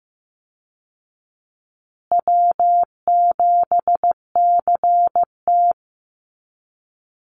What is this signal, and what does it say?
Morse code, heard as a keyed tone, about 14 wpm, 700 Hz. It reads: W7CT
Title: Easy Call Signs Repeated 3x - 15wpm